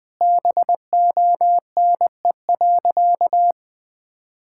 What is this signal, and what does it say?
Morse code, heard as a keyed tone, about 21 wpm, 700 Hz. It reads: BONE.
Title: Sentences from Top 1600 Words 20wpm